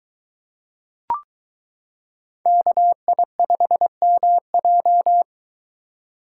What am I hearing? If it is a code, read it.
Morse code, heard as a keyed tone, about 23 wpm, 700 Hz. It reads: KI5MJ